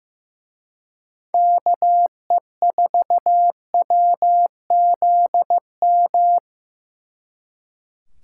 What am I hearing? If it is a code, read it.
Morse code, heard as a keyed tone, about 18 wpm, 700 Hz. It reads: KE4WZM
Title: Speed Racing - USA Call Signs 15wpm